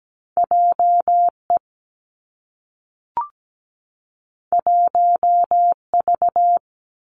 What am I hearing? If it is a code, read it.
Morse code, heard as a keyed tone, about 17 wpm, 700 Hz. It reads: JE 1V
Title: Two Challenging Letter-Number Combinations 17wpm